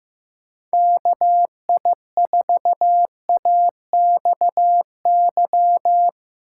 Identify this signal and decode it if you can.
Morse code, heard as a keyed tone, about 15 wpm, 700 Hz. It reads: KI4AXY
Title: USA Call Signs - 6 Characters - 2x Character Spacing - 15wpm